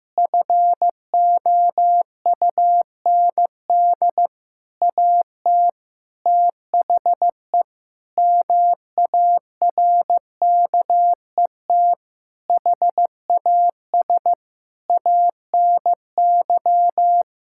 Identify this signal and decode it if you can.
Morse code, heard as a keyed tone, about 15 wpm, 700 Hz. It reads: FOUND AT THE MARKET HAS ANY
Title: Sentences from Top 1000 Words - Review Part 6 - 15wpm